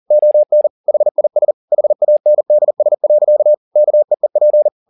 Morse code, heard as a keyed tone, about 40 wpm, 600 Hz. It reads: ON HIS HANDS. KEEP
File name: the old man and the sea 2 40wpm